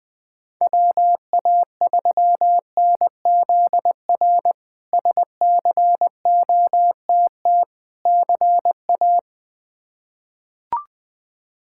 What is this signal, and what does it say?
Morse code, heard as a keyed tone, about 20 wpm, 700 Hz. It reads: WA3NZR SCOTT CA
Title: K1USN SST QSO Practice - 20wpm